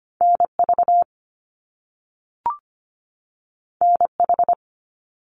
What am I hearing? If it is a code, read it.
Morse code, heard as a keyed tone, about 25 wpm, 700 Hz. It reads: NV NH
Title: US States 25wpm